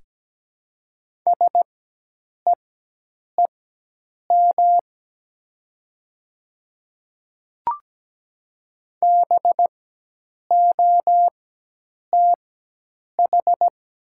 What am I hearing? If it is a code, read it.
Morse code, heard as a keyed tone, about 17 wpm, spaced as for 8 wpm, 700 Hz. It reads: SEEM BOTH
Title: 4x Character Spacing - Top 300 Words - 17wpm